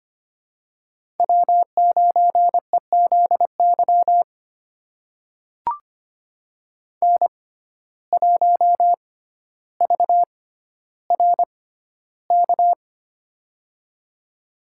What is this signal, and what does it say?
Morse code, heard as a keyed tone, about 25 wpm, spaced as for 13 wpm, 700 Hz. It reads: W9EZY N1VRK
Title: USA Call Signs - 5 Characters - 6x Character Spacing - 25wpm